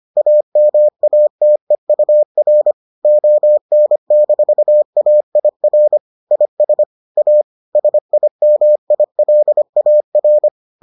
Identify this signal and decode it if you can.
Morse code, heard as a keyed tone, about 25 wpm, 600 Hz. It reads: AMATEUR ON-AIR IS A SIMILAR